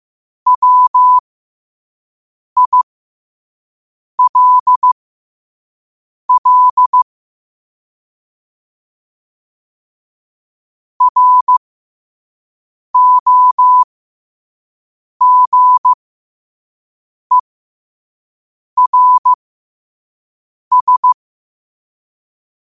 Morse code, heard as a keyed tone, about 15 wpm, spaced as for 5 wpm, 1000 Hz. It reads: WILL ROGERS
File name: qotdfiles / 5wpm